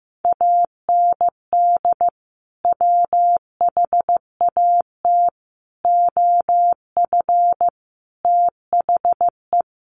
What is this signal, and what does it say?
Morse code, heard as a keyed tone, about 15 wpm, 700 Hz. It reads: AND WHAT OF THE
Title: Sentences from Top 1500 Words 15wpm